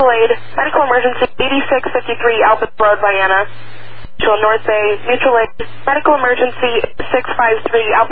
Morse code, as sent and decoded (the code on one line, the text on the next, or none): none